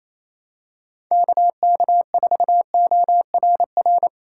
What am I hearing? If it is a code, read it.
Morse code, heard as a keyed tone, about 27 wpm, 700 Hz. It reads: KK4ORR
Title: USA Call Signs - 6 Characters - 6x Character Spacing - 28wpm